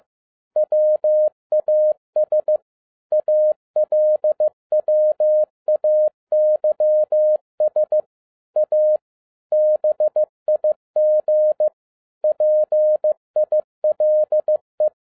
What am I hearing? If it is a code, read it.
Morse code, heard as a keyed tone, about 15 wpm, 610 Hz. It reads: WAS ALWAYS A BIG PILE